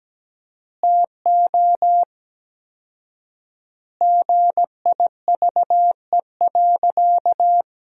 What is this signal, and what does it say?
Morse code, heard as a keyed tone, about 17 wpm, 700 Hz. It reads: TO GIVE.
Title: 4x Word-Spacing - Sentences from Top 200 Words - 17wpm